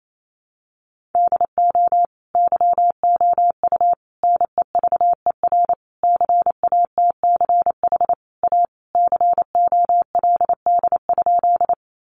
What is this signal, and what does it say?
Morse code, heard as a keyed tone, about 28 wpm, 700 Hz. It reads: DO YOU NEVER CATCH A COLD?
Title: Sentences from Top 1500 Words 28wpm